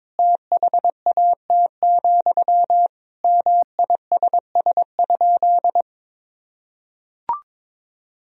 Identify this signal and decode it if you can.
Morse code, heard as a keyed tone, about 22 wpm, 700 Hz. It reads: THAT, MISS?
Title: Sentences from Top 500 Words and Abbreviations 22wpm